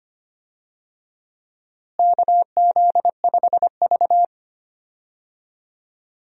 Morse code, heard as a keyed tone, about 25 wpm, 700 Hz. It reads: KZ5V
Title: USA Call Signs - 4 Characters - Repeated 3x - 25wpm